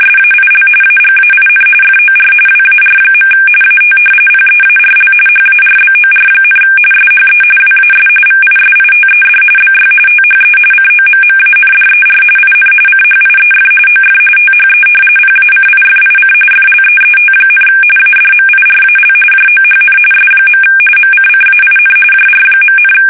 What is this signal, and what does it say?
RTTY, about 45.45 baud. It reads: SAIL AGAIN FOR WAR. AT PEARL HARBOR, SHE WAS VISITED BY THE COMMANDER IN CHIEF OF THE PACIFIC FLEET, ADMIRAL JOHN S. MCCAIN JR.,
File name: fsk850 / battleshipnj